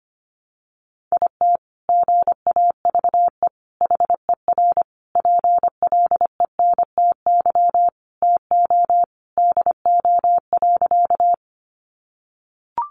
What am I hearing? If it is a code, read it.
Morse code, heard as a keyed tone, about 25 wpm, 700 Hz. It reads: IT GAVE HER PLENTY TO DO.
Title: Sentences from Top 1300 Words 25wpm